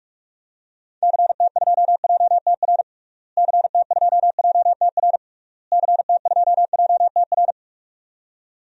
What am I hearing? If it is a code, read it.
Morse code, heard as a keyed tone, about 41 wpm, 700 Hz. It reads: CT2JTR CT2JTR CT2JTR
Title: DXCC-100 Call Signs - 6 Characters - Repeated 3x - 45wpm